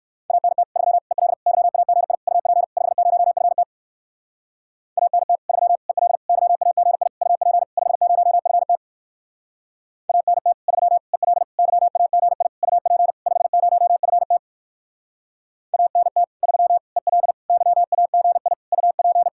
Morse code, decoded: ANT 3 EL YAGI UP 50FT ANT 3 EL YAGI UP 50FT ANT 3 EL YAGI UP 50FT ANT 3 EL YAGI UP

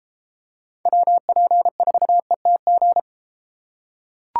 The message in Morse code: WP4ETG

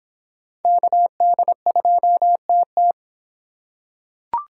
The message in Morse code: KD2TT

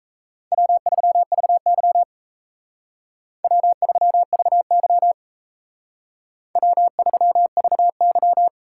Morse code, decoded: W3VY W3VY W3VY

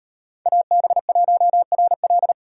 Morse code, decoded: AB1RL

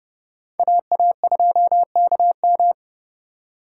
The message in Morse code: AA2KM

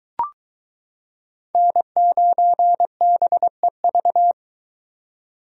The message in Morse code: N9BEV

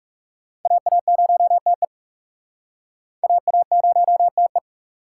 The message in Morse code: AA0TE AA0TE

